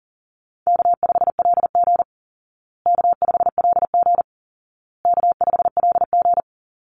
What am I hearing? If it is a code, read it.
Morse code, heard as a keyed tone, about 40 wpm, 700 Hz. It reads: K5LG K5LG K5LG
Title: Easy Call Signs Repeated 3x - 40wpm